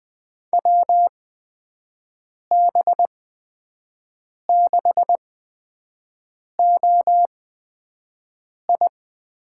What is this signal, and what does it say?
Morse code, heard as a keyed tone, about 20 wpm, spaced as for 5 wpm, 700 Hz. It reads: WB6OI
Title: USA Call Signs - 6 Characters - 8x Character Spacing - 20wpm